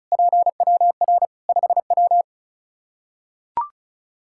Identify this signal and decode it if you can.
Morse code, heard as a keyed tone, about 36 wpm, 700 Hz. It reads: PWR 5W
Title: Practice QSO - Exchange 2 - Info - 35wpm